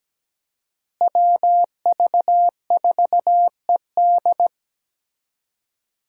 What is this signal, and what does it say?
Morse code, heard as a keyed tone, about 17 wpm, 700 Hz. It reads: WV4ED
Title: USA Call Signs - 5 Characters - Repeated 2x - 17wpm